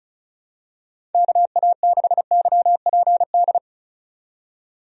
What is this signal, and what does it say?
Morse code, heard as a keyed tone, about 36 wpm, 700 Hz. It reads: KA6YPD
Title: USA Call Signs - 6 Characters - 35wpm